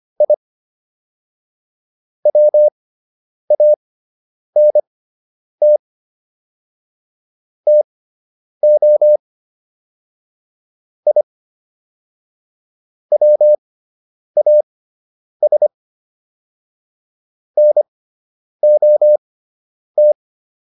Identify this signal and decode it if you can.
Morse code, heard as a keyed tone, about 25 wpm, spaced as for 9 wpm, 600 Hz. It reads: I WANT TO I WAS NOT